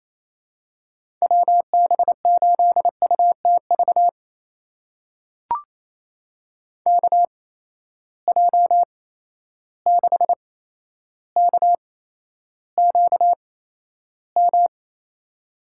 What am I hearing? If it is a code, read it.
Morse code, heard as a keyed tone, about 28 wpm, spaced as for 12 wpm, 700 Hz. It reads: WB8UTV KJ6KQM
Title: USA Call Signs - 6 Characters - 8x Character Spacing - 28wpm